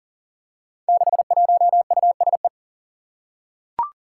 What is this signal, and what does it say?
Morse code, heard as a keyed tone, about 38 wpm, 700 Hz. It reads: 61USE
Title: Five Challenging Letter-Number Combinations 40wpm